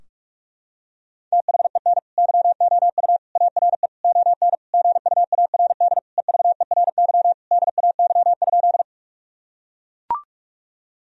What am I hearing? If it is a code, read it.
Morse code, heard as a keyed tone, about 41 wpm, 700 Hz. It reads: THEN YOU ARE ON GUARD EVERY DAY?
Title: Sentences from Top 1000 Words - Review Part 4 - 45wpm